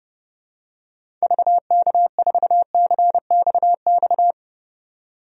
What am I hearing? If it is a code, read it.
Morse code, heard as a keyed tone, about 30 wpm, 700 Hz. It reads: VK4CXX
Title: DXCC-100 Call Signs - 6 Characters - 30wpm